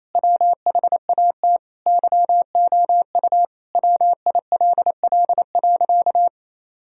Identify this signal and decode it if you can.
Morse code, heard as a keyed tone, about 27 wpm, 700 Hz. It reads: WHAT YOU WILL.
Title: Sentences from Top 1000 Words - Review Part 2 - 28wpm